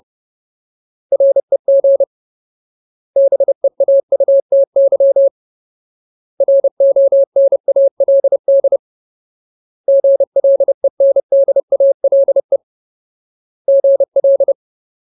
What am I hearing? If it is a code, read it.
Morse code, heard as a keyed tone, about 30 wpm, 550 Hz. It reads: REG BEAUTY RONALD GLENDALE GL
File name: Ultimate Word List / 30WPM Ultimate